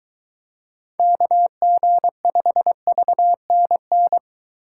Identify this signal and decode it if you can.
Morse code, heard as a keyed tone, about 21 wpm, 700 Hz. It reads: KG5VNN